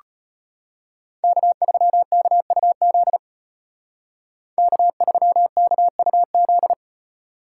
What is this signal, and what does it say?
Morse code, heard as a keyed tone, about 32 wpm, 700 Hz. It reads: K3KUZ K3KUZ